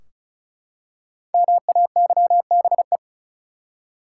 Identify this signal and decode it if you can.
Morse code, heard as a keyed tone, about 34 wpm, 700 Hz. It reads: MAYBE
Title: Five Letter Words - 35wpm